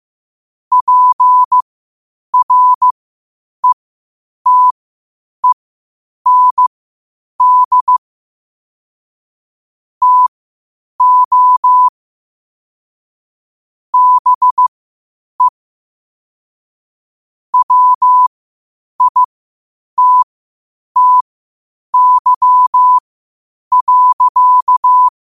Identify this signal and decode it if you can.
Morse code, heard as a keyed tone, about 15 wpm, spaced as for 8 wpm, 1000 Hz. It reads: PRETEND TO BE WITTY.
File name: qotdfiles / 8wpm